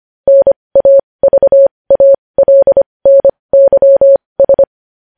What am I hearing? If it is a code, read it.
Morse code, heard as a keyed tone, about 25 wpm, spaced as for 20 wpm, 550 Hz. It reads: NAVALNYS
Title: News Headlines at 20